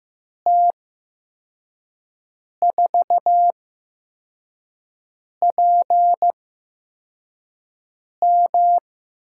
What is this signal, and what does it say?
Morse code, heard as a keyed tone, about 15 wpm, 700 Hz. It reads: T 4 P M
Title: USA Call Signs - 5 Characters - 8x Character Spacing - 15wpm